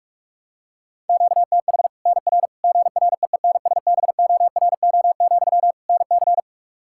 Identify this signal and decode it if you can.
Morse code, heard as a keyed tone, about 46 wpm, 700 Hz. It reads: QTH NR GREENSBORO, NC
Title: Practice QSO - Exchange 1 - Beginning - 45wpm